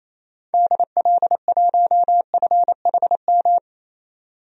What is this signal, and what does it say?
Morse code, heard as a keyed tone, about 28 wpm, 700 Hz. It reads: DL1FHM
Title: DXCC-100 Call Signs - 6 Characters - 28wpm